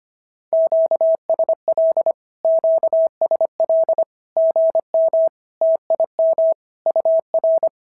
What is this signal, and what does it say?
Morse code, heard as a keyed tone, about 25 wpm, 690 Hz. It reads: QSL QSL GM TIM UR
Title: SOTA QSO Practice 25wpm